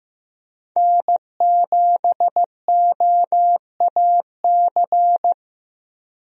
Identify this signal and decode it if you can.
Morse code, heard as a keyed tone, about 15 wpm, 700 Hz. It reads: N7OAC
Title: USA Call Signs - 5 Characters - 15wpm